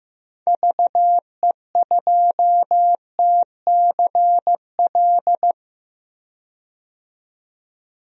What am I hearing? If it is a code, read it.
Morse code, heard as a keyed tone, about 15 wpm, 700 Hz. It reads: VE2TCL
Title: DXCC-100 Call Signs - 6 Characters - 15wpm